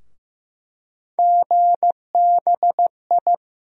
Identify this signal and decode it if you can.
Morse code, heard as a keyed tone, about 15 wpm, 700 Hz. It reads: GBI
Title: USA Call Sign Postfixes - 3 Letters 15wpm